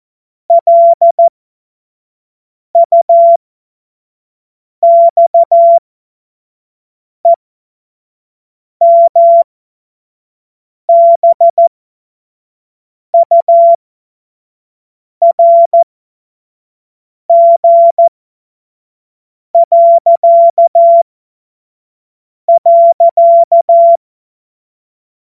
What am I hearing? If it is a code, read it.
Morse code, heard as a keyed tone, about 14 wpm, spaced as for 5 wpm, 680 Hz. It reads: LUXEMBURG..
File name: Morsekurs / Witze / Witz 03 Juni 2025 25 F70